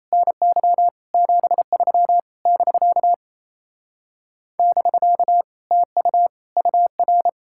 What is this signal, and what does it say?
Morse code, heard as a keyed tone, about 37 wpm, 700 Hz. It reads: NY 73 <BK> <BK> TU UR